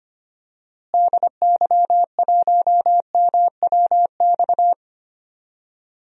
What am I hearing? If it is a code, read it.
Morse code, heard as a keyed tone, about 25 wpm, 700 Hz. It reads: DY1MWX
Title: DXCC-100 Call Signs - 6 Characters - Repeated 3x - 25wpm